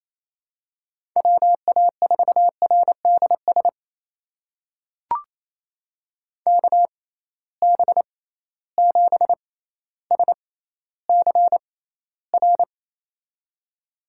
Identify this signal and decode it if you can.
Morse code, heard as a keyed tone, about 27 wpm, spaced as for 14 wpm, 700 Hz. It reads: WA4RDS KB7SCR
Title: USA Call Signs - 6 Characters - 6x Character Spacing - 28wpm